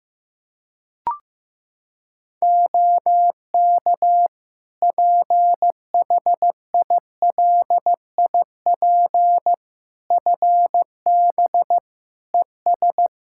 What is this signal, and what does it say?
Morse code, heard as a keyed tone, about 15 wpm, 700 Hz. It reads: OK PHILIP FB ES
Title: K1USN SST QSO Practice 15wpm